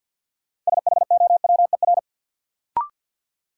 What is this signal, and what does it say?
Morse code, heard as a keyed tone, about 52 wpm, 700 Hz. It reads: SHOWER
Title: Sentences from Top 1600 Words - New Word Drill - 50wpm